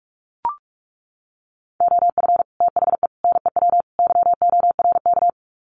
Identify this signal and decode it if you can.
Morse code, heard as a keyed tone, about 44 wpm, 700 Hz. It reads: OF THE NEW YORK
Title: Sets of 4 Words 45wpm